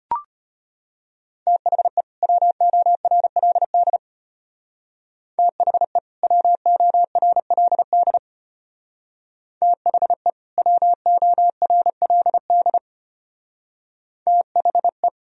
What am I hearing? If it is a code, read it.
Morse code, heard as a keyed tone, about 31 wpm, 700 Hz. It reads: THE WORLD THE WORLD THE WORLD THE